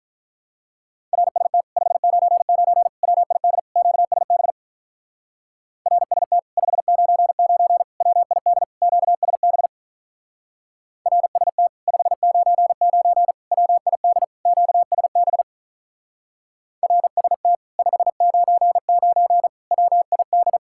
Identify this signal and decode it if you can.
Morse code, heard as a keyed tone, about 43 wpm, 700 Hz. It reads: RST 599 WID QSB RST 599 WID QSB RST 599 WID QSB RST 599 WID
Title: Speed-Racing Common QSO Phrases - 35wpm